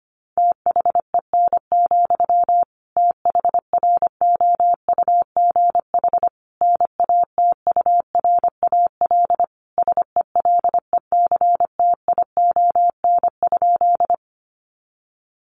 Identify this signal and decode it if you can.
Morse code, heard as a keyed tone, about 25 wpm, 700 Hz. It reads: THEN, THROUGH NATURAL SELECTION?
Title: Sentences from Top 1400 Words 25wpm